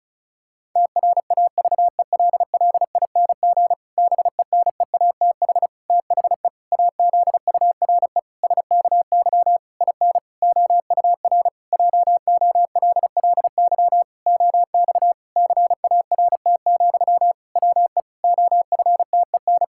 Morse code, heard as a keyed tone, about 34 wpm, 700 Hz. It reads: TRAVELLING BENEATH THE AZURE SKY IN OUR JOLLY OX CART, WE OFTEN
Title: Pangram Challenge v2 - 35wpm